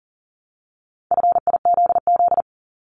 Morse code, heard as a keyed tone, about 40 wpm, 700 Hz. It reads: FIZZ
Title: Multi-letter Phonemes Repeated 3x 40wpm